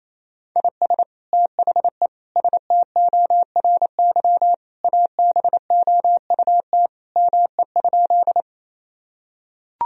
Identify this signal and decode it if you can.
Morse code, heard as a keyed tone, about 28 wpm, 700 Hz. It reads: IS THE STORY ABOUT ME?
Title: Sentences from Top 1000 Words - Review Part 1 - 28wpm